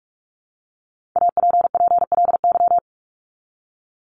Morse code, heard as a keyed tone, about 44 wpm, 700 Hz. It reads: APPLY
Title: Top 1001-1250 Words 45wpm